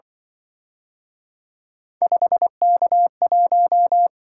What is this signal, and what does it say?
Morse code, heard as a keyed tone, about 25 wpm, 700 Hz. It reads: 5K1
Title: Speed-Racing - Three Letter-Number Combinations - 20wpm